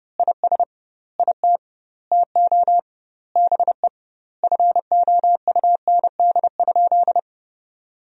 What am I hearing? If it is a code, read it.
Morse code, heard as a keyed tone, about 30 wpm, 700 Hz. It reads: IS IT TO BE FOUND?
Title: 2x Word-Spacing - Sentences from Top 300 Words - 30wpm